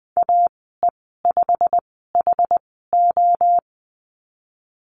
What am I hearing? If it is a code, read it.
Morse code, heard as a keyed tone, about 20 wpm, spaced as for 15 wpm, 700 Hz. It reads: AE5HO
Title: 193 Lesson 31g - Letter v - Call Signs 20wpm